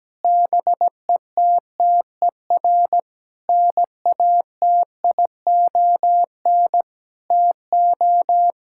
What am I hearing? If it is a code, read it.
Morse code, heard as a keyed tone, about 17 wpm, 700 Hz. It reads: BETTER NATION TO
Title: Sentences from Top 1000 Words - Review Part 6 - 17wpm